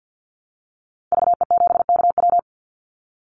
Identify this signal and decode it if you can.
Morse code, heard as a keyed tone, about 48 wpm, 700 Hz. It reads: VE7KW
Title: Call Signs Repeated 2x - 50wpm